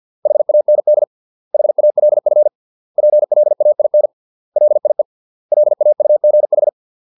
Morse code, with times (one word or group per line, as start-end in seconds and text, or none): 0.3-1.0 HAND
1.5-2.5 HALF
3.0-4.1 PLAIN
4.6-5.0 LIE
5.5-6.7 LAUGH